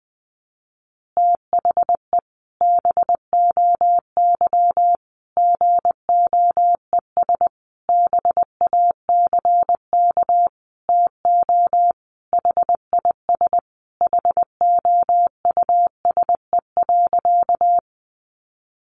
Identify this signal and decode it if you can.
Morse code, heard as a keyed tone, about 20 wpm, 700 Hz. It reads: THE BOY GOES BACK TO HIS HOUSE.
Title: Easy Phrases 20wpm